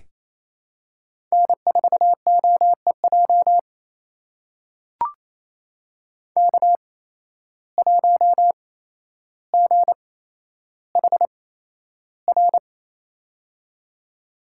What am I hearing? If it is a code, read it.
Morse code, heard as a keyed tone, about 28 wpm, spaced as for 12 wpm, 700 Hz. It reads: N4OEJ K1GHR
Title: USA Call Signs - 5 Characters - 8x Character Spacing - 28wpm